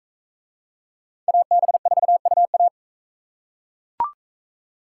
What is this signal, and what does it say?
Morse code, heard as a keyed tone, about 39 wpm, 700 Hz. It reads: AB4UA